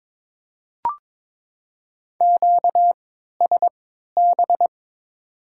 Morse code, heard as a keyed tone, about 21 wpm, spaced as for 12 wpm, 700 Hz. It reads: QSB
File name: Top 90 QSO Elements - 3x Character Spacing - 22wpm